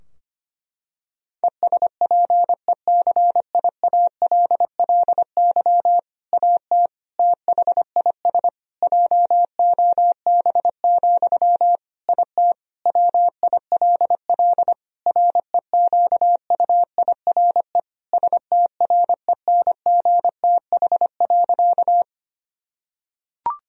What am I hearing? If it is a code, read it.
Morse code, heard as a keyed tone, about 25 wpm, 700 Hz. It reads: ESPECIALLY AT THIS JOB, IT WILL REQUIRE STRENGTH.
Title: Sentences from Top 1000 Words - Review Part 4 - 25wpm